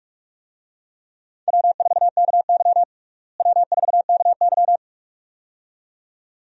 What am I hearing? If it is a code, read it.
Morse code, heard as a keyed tone, about 46 wpm, 700 Hz. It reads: W4KY W4KY
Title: USA Call Signs - 4 Characters - Repeated 2x - 45wpm